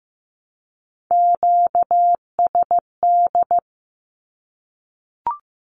Morse code, heard as a keyed tone, about 15 wpm, 700 Hz. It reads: QSD
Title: Q-Codes Repeated 3x 15wpm